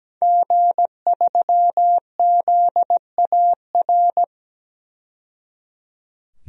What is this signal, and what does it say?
Morse code, heard as a keyed tone, about 17 wpm, 700 Hz. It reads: G3ZAR
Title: DXCC-100 Call Signs - 5 Characters - 17wpm